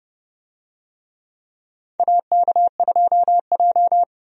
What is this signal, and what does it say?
Morse code, heard as a keyed tone, about 30 wpm, 700 Hz. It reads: AK2J
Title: USA Call Signs - 4 Characters - 30wpm